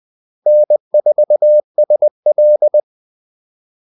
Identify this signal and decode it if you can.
Morse code, heard as a keyed tone, about 20 wpm, 600 Hz. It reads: N4SL
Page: Kaldesignaler 20 wpm